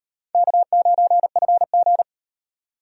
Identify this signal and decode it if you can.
Morse code, heard as a keyed tone, about 38 wpm, 700 Hz. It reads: K9FG